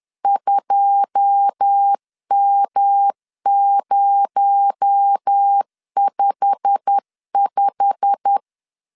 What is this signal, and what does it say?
Morse code, heard as a keyed tone, about 10 wpm, 800 Hz. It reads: 2M055